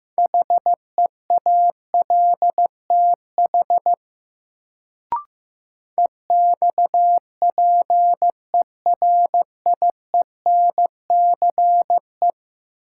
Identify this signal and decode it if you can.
Morse code, heard as a keyed tone, about 15 wpm, 700 Hz. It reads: HEALTH EXPERIENCE
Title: Sentences from Top 600 Words - New Word Drill - 15wpm